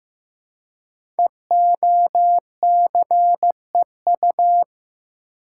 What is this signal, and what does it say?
Morse code, heard as a keyed tone, about 15 wpm, 700 Hz. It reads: EOCEU
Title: Code Groups - 5 Letter:Numbers - 15wpm